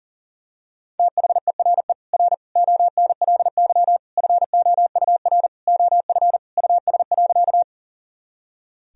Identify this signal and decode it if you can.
Morse code, heard as a keyed tone, about 40 wpm, 700 Hz. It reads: THERE R ONLY FOUR OF US.
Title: Sentences from Top 300 Words and Abbreviations 40wpm